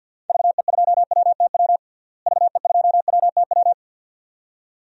Morse code, heard as a keyed tone, about 51 wpm, 700 Hz. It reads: VE2WTW VE2WTW
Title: DXCC-100 Call Signs - 6 Characters - Repeated 2x - 50wpm